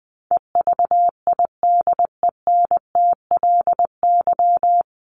Morse code, heard as a keyed tone, about 20 wpm, 700 Hz. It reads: EVIDENTLY
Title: LY Suffix - Top 100 Words with Base Words 20wpm